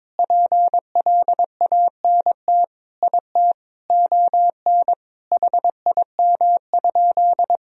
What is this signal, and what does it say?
Morse code, heard as a keyed tone, about 22 wpm, 700 Hz. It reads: PLANT IT ON HIM?
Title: Sentences from Top 1000 Words - Review Part 1 - 22wpm